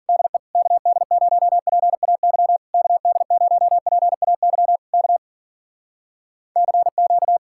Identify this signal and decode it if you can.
Morse code, heard as a keyed tone, about 52 wpm, 700 Hz. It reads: DE KD0PAY KD0PAY K CQ